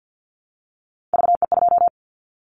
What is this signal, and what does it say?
Morse code, heard as a keyed tone, about 50 wpm, 700 Hz. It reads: VE2
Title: Callsign Prefixes for Top 25 DXCC - 50wpm